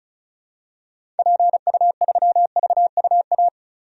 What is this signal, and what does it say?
Morse code, heard as a keyed tone, about 34 wpm, 700 Hz. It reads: PU3VUA
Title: DXCC-100 Call Signs - 6 Characters - 35wpm